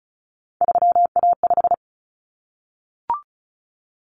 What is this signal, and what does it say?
Morse code, heard as a keyed tone, about 34 wpm, 700 Hz. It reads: 3A5
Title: Common Cut Numbers - 3-Digit 35wpm